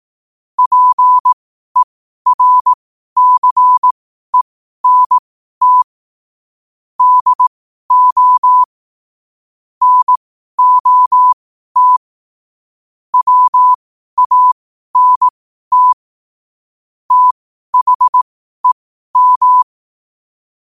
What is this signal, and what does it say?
Morse code, heard as a keyed tone, about 18 wpm, spaced as for 12 wpm, 1000 Hz. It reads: PERCENT DO NOT WANT THEM
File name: qotdfiles / 12wpm